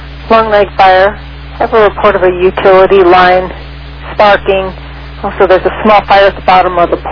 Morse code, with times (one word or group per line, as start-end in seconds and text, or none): none